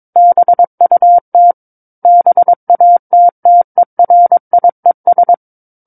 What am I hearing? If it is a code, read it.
Morse code, heard as a keyed tone, about 22 wpm, 700 Hz. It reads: BUT BATTERIES
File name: Book One / 13-Chapter 13 - 22 WPM